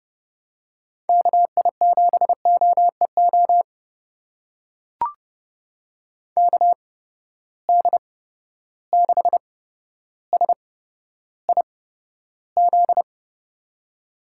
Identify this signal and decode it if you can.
Morse code, heard as a keyed tone, about 31 wpm, spaced as for 13 wpm, 700 Hz. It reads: KI7OEO KD6SIZ
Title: USA Call Signs - 6 Characters - 8x Character Spacing - 30wpm